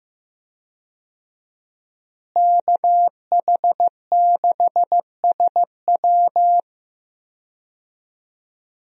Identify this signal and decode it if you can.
Morse code, heard as a keyed tone, about 15 wpm, 700 Hz. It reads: KH6SW
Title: DXCC-100 Call Signs - 5 Characters - Repeated 2x - 15wpm